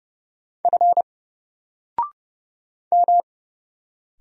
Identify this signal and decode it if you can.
Morse code, heard as a keyed tone, about 30 wpm, 700 Hz. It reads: F M
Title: Single Letter-Number 30wpm